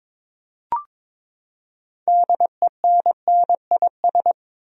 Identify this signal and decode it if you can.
Morse code, heard as a keyed tone, about 22 wpm, 700 Hz. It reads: DENNIS
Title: USA Common Names - 22wpm